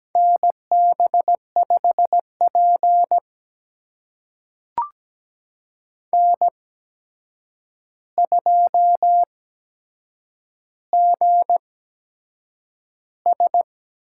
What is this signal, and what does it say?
Morse code, heard as a keyed tone, about 17 wpm, spaced as for 7 wpm, 700 Hz. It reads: NB5P N2GS
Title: USA Call Signs - 4 Characters - 8x Character Spacing - 17wpm